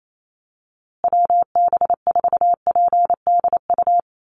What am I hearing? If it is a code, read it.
Morse code, heard as a keyed tone, about 28 wpm, 700 Hz. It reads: WB4PDU
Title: USA Call Signs Repeated 2x 28wpm